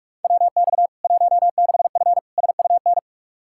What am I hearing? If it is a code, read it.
Morse code, heard as a keyed tone, about 41 wpm, 700 Hz. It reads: WX 16F SUN
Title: K1USN SST QSO Practice 45wpm